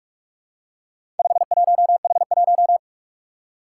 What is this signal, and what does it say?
Morse code, heard as a keyed tone, about 44 wpm, 700 Hz. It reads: 51H1
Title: Four Challenging Letter-Number Combinations 45wpm